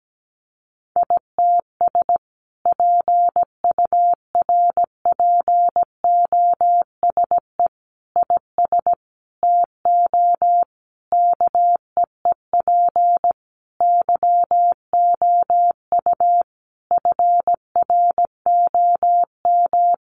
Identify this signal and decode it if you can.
Morse code, heard as a keyed tone, about 17 wpm, 700 Hz. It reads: ITS PURPOSE IS TO KEEP YOU FROM
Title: A Sound Of Thunder 17wpm